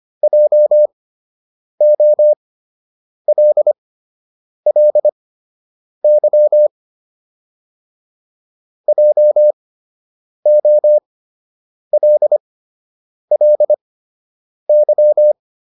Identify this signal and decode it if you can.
Morse code, heard as a keyed tone, about 25 wpm, spaced as for 8 wpm, 600 Hz. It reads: JOLLY JOLLY